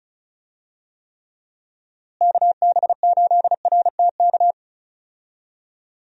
Morse code, heard as a keyed tone, about 36 wpm, 700 Hz. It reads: KB8RTK
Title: USA Call Signs - 6 Characters - Repeated 2x - 35wpm